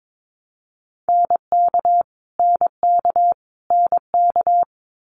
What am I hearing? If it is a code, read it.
Morse code, heard as a keyed tone, about 22 wpm, 700 Hz. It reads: NK NK NK
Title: Multi-letter Phonemes Repeated 3x 22wpm